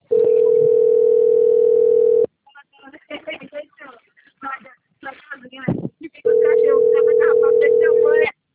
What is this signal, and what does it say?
Ringback tone (North America), 440 + 480 Hz, with speech.
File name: out- - - - - .29794